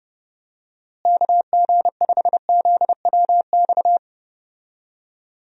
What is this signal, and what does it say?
Morse code, heard as a keyed tone, about 31 wpm, 700 Hz. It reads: KG5ZWX
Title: USA Call Signs - 6 Characters - Repeated 2x - 30wpm